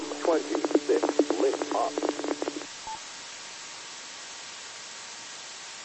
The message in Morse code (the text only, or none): none